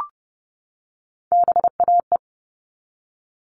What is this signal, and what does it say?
Morse code, heard as a keyed tone, about 30 wpm, 700 Hz. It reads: BAE
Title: All Cut Numbers - 3-Digit 30wpm